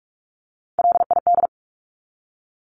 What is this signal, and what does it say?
Morse code, heard as a keyed tone, about 44 wpm, 700 Hz. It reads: LID
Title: Abbreviations 45wpm